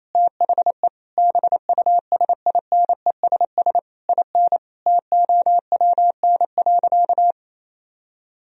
Morse code, heard as a keyed tone, about 27 wpm, 700 Hz. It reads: THE BUSINESS IN TOWN.
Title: Sentences from Top 1000 Words - Review Part 2 - 28wpm